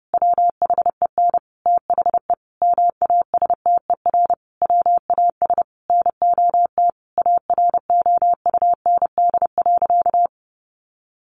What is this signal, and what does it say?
Morse code, heard as a keyed tone, about 30 wpm, 700 Hz. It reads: WHEN THE MASTER WAS NOT AROUND.
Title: Sentences from Top 1300 Words 30wpm